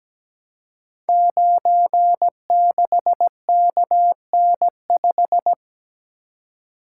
Morse code, heard as a keyed tone, about 17 wpm, 700 Hz. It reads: 96KN5
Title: Code Groups - 5 Letter:Numbers - 17wpm